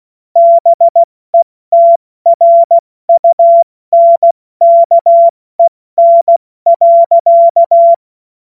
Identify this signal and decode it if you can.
Morse code, heard as a keyed tone, about 16 wpm, 680 Hz. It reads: BETRUNKEN.
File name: Morsekurs / Witze / Witz 28. Dezember 2021, 70 F80